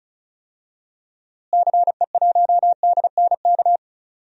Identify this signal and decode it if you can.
Morse code, heard as a keyed tone, about 36 wpm, 700 Hz. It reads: CE1DNK
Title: DXCC-100 Call Signs - 6 Characters - Repeated 3x - 35wpm